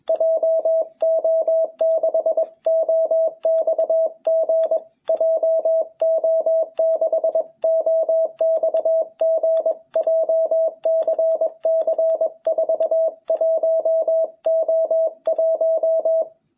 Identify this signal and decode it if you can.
Morse code, heard as a keyed tone, about 21 wpm, 660 Hz. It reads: JO6OXG JO6OXG JCC 41O1
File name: jh8cbh / 2021cw